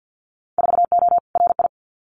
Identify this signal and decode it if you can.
Morse code, heard as a keyed tone, about 50 wpm, 700 Hz. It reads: 4O RI
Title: Winter Field Day Exchanges Encore 50wpm